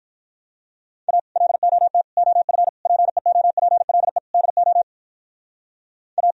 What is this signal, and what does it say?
Morse code, heard as a keyed tone, about 43 wpm, 700 Hz. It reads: A LOT OF PEOPLE DO A
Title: Speed-Racing - Sets of 5 Words 35wpm